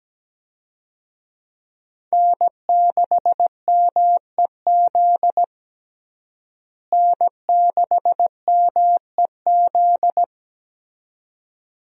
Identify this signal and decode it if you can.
Morse code, heard as a keyed tone, about 17 wpm, 700 Hz. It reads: N6MEZ N6MEZ
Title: USA Call Signs - 5 Characters - Repeated 2x - 17wpm